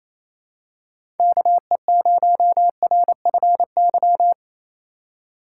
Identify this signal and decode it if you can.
Morse code, heard as a keyed tone, about 28 wpm, 700 Hz. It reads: KE0RFY
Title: USA Call Signs - 6 Characters - 2x Character Spacing - 28wpm